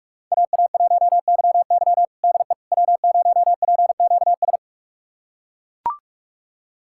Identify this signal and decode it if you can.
Morse code, heard as a keyed tone, about 41 wpm, 700 Hz. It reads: AA1YY DE W0PQS
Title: USA Callsign DE Callsign - 5 Characters - 45wpm